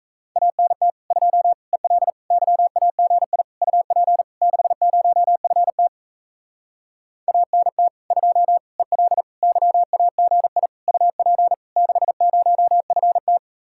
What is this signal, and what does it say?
Morse code, heard as a keyed tone, about 38 wpm, 700 Hz. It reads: ANT 2 EL YAGI UP 60FT ANT 2 EL YAGI UP 60FT